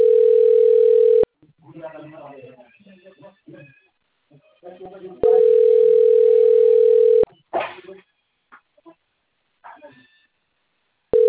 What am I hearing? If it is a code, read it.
Ringback tone (North America), 440 + 480 Hz, with speech.
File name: rg-1600-unknown- - - .23568